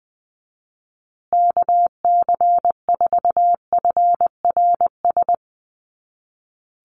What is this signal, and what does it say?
Morse code, heard as a keyed tone, about 20 wpm, 700 Hz. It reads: KC4FRS
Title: USA Call Signs Repeated 3x 20wpm